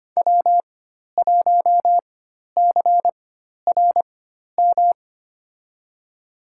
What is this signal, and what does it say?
Morse code, heard as a keyed tone, about 25 wpm, spaced as for 12 wpm, 700 Hz. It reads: W1CRM
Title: USA Call Signs - 4x Character Spacing - 25wpm